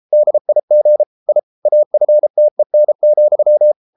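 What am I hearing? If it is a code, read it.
Morse code, heard as a keyed tone, about 33 wpm, 600 Hz. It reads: DIG I AFTEN,
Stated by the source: Morse code at 33 wpm, QRQ.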